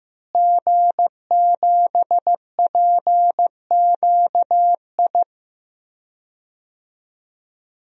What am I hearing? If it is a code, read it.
Morse code, heard as a keyed tone, about 15 wpm, 700 Hz. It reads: G7PQI
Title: DXCC-100 Call Signs - 5 Characters - Repeated 2x - 15wpm